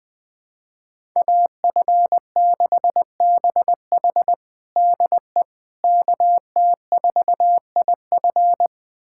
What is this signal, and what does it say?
Morse code, heard as a keyed tone, about 21 wpm, 700 Hz. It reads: AF6BH DE KT4IF
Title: USA Callsign DE Callsign - 5 Characters - 20wpm